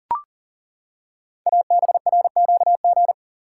Morse code, heard as a keyed tone, about 39 wpm, 690 Hz. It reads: ABRQG
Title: Code Groups - 5 Letters - 40wpm